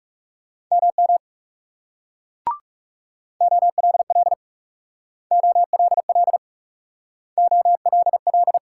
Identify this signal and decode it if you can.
Morse code, heard as a keyed tone, about 41 wpm, 700 Hz. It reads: MM OLL OLL OLL